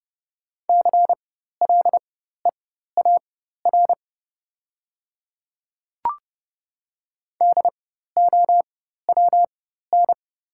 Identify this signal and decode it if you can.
Morse code, heard as a keyed tone, about 29 wpm, spaced as for 14 wpm, 690 Hz. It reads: CLEAR DOWN
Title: 4x Character Spacing - Top 500 Words - 30wpm